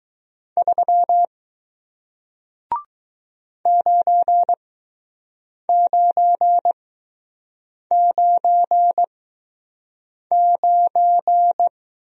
Morse code, heard as a keyed tone, about 20 wpm, spaced as for 6 wpm, 700 Hz. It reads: 3 9999